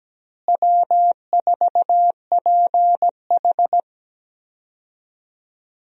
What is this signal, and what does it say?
Morse code, heard as a keyed tone, about 17 wpm, 700 Hz. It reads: W4PH
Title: USA Call Signs - 4 Characters - Repeated 2x - 17wpm